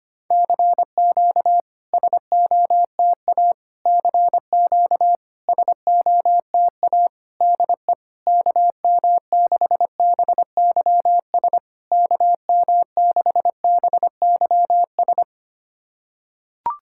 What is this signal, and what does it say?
Morse code, heard as a keyed tone, about 25 wpm, 690 Hz. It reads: CQ SOTA CQ SOTA DE KM6BYS KM6BYS
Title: SOTA QSO Practice 25wpm